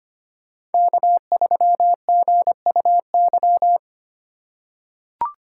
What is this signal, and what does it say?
Morse code, heard as a keyed tone, about 25 wpm, 700 Hz. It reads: K3GUY
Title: USA Call Signs - 5 Characters - 3x Character Spacing - 25wpm